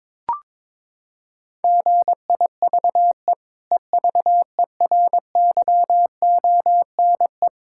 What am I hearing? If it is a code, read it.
Morse code, heard as a keyed tone, about 21 wpm, 700 Hz. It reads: GIVE EVERYONE
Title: Sentences from Top 1000 Words - Review Part 6 - 22wpm